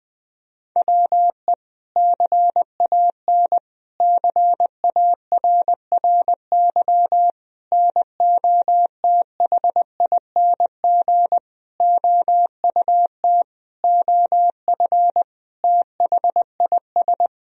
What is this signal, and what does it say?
Morse code, heard as a keyed tone, about 20 wpm, 700 Hz. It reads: WE CAN CARRY NOTHING OUT OF THIS
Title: Sentences from Top 1000 Words - Review Part 1 - 20wpm